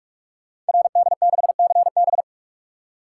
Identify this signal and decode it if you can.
Morse code, heard as a keyed tone, about 41 wpm, 700 Hz. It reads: RD6CB
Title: Code Groups - 5 Letter:Numbers - 45wpm